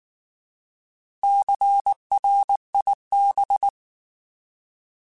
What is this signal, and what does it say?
Morse code, heard as a keyed tone, about 19 wpm, 800 Hz. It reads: CRIB